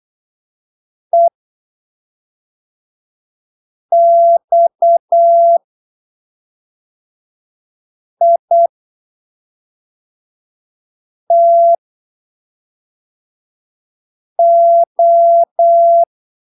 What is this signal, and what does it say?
Morse code, heard as a keyed tone, about 8 wpm, spaced as for 3 wpm, 680 Hz. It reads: EXITO